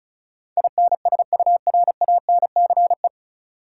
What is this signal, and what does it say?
Morse code, heard as a keyed tone, about 36 wpm, 700 Hz. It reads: INSURANCE
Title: Sentences from Top 1100 Words - New Word Drill - 35wpm